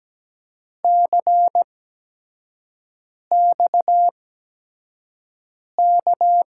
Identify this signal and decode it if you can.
Morse code, heard as a keyed tone, about 17 wpm, spaced as for 5 wpm, 700 Hz. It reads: CXK
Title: USA Call Signs - 6 Characters - 8x Character Spacing - 17wpm